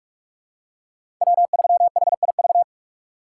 Morse code, heard as a keyed tone, about 47 wpm, 690 Hz. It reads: W3HIV
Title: USA Call Signs - 5 Characters - 2x Character Spacing - 45wpm